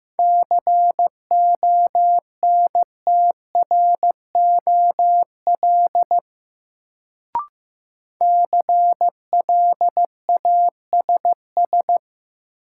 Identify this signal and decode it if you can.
Morse code, heard as a keyed tone, about 15 wpm, 700 Hz. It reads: CONTROL CLASS
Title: Sentences from Top 1000 Words - Review Part 2 - New Word Drill - 15wpm